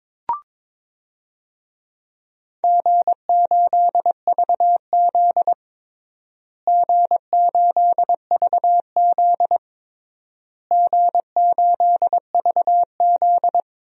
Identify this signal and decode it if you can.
Morse code, heard as a keyed tone, about 22 wpm, 700 Hz. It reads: G8VZ G8VZ G8VZ
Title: DXCC-100 Call Signs - 4 Characters - Repeated 3x - 22wpm